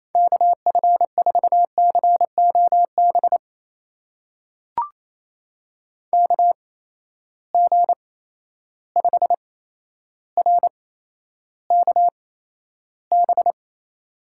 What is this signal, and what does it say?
Morse code, heard as a keyed tone, about 28 wpm, spaced as for 12 wpm, 700 Hz. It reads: KF4COB KG5RKB
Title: USA Call Signs - 6 Characters - 8x Character Spacing - 28wpm